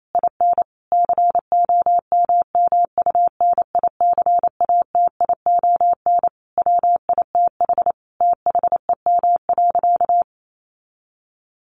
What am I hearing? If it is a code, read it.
Morse code, heard as a keyed tone, about 28 wpm, 700 Hz. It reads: IN COMMUNICATION WITH THEM.
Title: Sentences from Top 1400 Words 28wpm